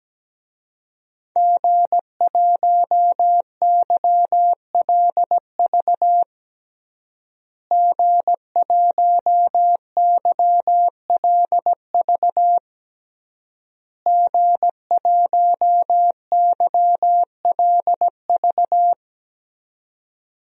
Morse code, heard as a keyed tone, about 17 wpm, 700 Hz. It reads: G1YLV G1YLV G1YLV
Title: DXCC-100 Call Signs - 5 Characters - Repeated 3x - 17wpm